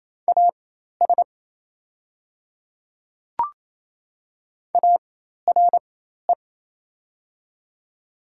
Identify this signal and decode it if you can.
Morse code, heard as a keyed tone, about 28 wpm, spaced as for 13 wpm, 700 Hz. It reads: AS ARE
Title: 4x Character Spacing - Top 100 Words - 28wpm